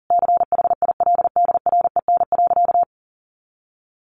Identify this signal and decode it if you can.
Morse code, heard as a keyed tone, about 40 wpm, 700 Hz. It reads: CHILDREN.
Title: Sentences from Top 1200 Words 40wpm